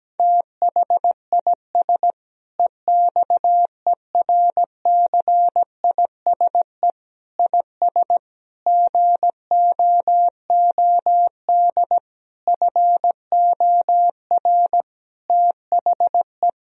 Morse code, heard as a keyed tone, about 17 wpm, 700 Hz. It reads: THIS EXERCISE IS GOOD FOR THE
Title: Sentences from Top 1000 Words - Review Part 4 - 17wpm